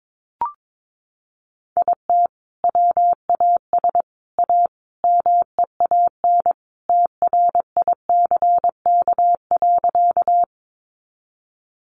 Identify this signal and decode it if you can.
Morse code, heard as a keyed tone, about 22 wpm, 700 Hz. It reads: IT WAS A MEAN TRICK.
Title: Sentences from Top 1500 Words 22wpm